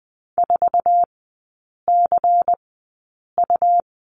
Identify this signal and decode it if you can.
Morse code, heard as a keyed tone, about 20 wpm, spaced as for 8 wpm, 700 Hz. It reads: 4CU
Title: 100 Lesson 17e - Letter u - Three Characters 20wpm